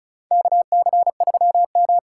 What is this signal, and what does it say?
Morse code, heard as a keyed tone, about 36 wpm, 700 Hz. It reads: KC3M
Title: USA Call Signs - 8x Character Spacing - 35wpm